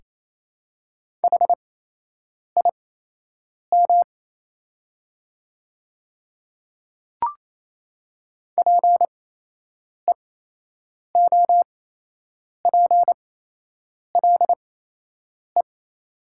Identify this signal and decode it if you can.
Morse code, heard as a keyed tone, about 28 wpm, spaced as for 8 wpm, 700 Hz. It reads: HIM PEOPLE
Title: 8x Character Spacing - Top 100 Words - 28wpm